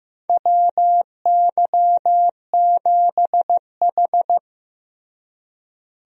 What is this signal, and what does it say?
Morse code, heard as a keyed tone, about 15 wpm, 700 Hz. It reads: WY7H
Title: USA Call Signs - 4 Characters - Repeated 3x - 15wpm